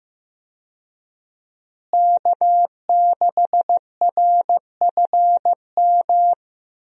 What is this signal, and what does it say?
Morse code, heard as a keyed tone, about 15 wpm, 700 Hz. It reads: K6RFM
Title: USA Call Signs - 5 Characters - Repeated 2x - 15wpm